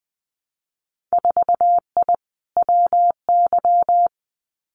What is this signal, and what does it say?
Morse code, heard as a keyed tone, about 20 wpm, 700 Hz. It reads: 4I WY
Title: Winter Field Day Exchanges Encore 20wpm